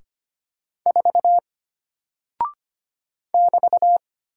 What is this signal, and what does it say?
Morse code, heard as a keyed tone, about 25 wpm, 700 Hz. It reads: 4 <BT>
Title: Single Character (Letter:Number:Punctuation:Prosign) Alt 25wpm